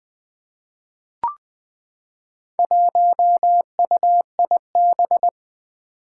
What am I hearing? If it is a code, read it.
Morse code, heard as a keyed tone, about 20 wpm, 700 Hz. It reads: 1UIB
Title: Four Challenging Letter-Number Combinations 20wpm